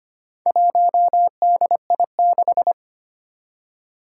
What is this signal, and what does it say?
Morse code, heard as a keyed tone, about 25 wpm, 700 Hz. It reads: 1DI6
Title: Four Challenging Letter-Number Combinations 25wpm